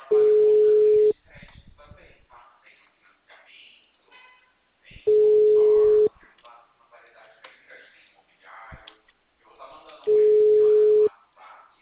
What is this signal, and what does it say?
Ringback tone (Europe, 425 Hz), with speech.
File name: out- -2012- - - .16265